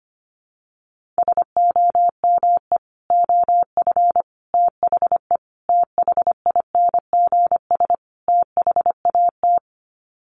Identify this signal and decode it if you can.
Morse code, heard as a keyed tone, about 25 wpm, 700 Hz. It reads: SOME OF THE THINGS THAT
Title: Sets of 5 Words 25wpm